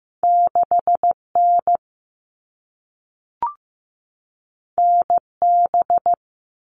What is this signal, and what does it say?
Morse code, heard as a keyed tone, about 15 wpm, 700 Hz. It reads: 6N NB
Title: All Cut Numbers - 2-Digit 15wpm